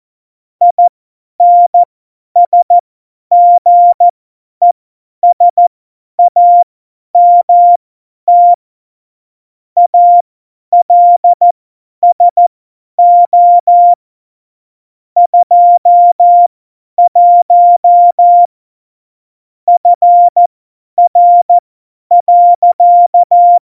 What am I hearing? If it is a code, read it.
Morse code, heard as a keyed tone, about 14 wpm, spaced as for 10 wpm, 700 Hz. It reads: INSGESAMT ALSO 21 FR.
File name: Morsekurs / Witze / Witz 05. Februar 2019, 50 F70